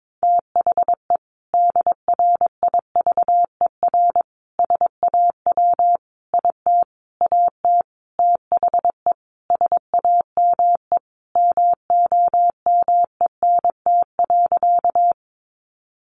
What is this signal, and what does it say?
Morse code, heard as a keyed tone, about 22 wpm, 700 Hz. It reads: THE DRIVER SAW IT AT THE SAME MOMENT.
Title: Sentences from Top 1000 Words 22wpm